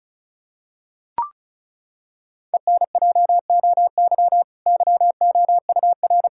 Morse code, heard as a keyed tone, about 36 wpm, 700 Hz. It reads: ENJOY YOUR
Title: Sentences from Top 1000 Words - Review Part 5 - 35wpm